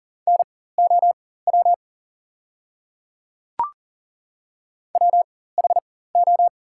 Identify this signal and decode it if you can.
Morse code, heard as a keyed tone, about 37 wpm, spaced as for 18 wpm, 700 Hz. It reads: NOW WHO
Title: 4x Character Spacing - Top 100 Words - 40wpm